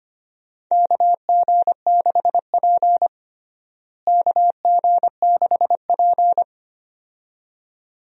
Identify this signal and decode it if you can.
Morse code, heard as a keyed tone, about 25 wpm, 700 Hz. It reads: KG6P KG6P
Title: USA Call Signs - 4 Characters - Repeated 2x - 25wpm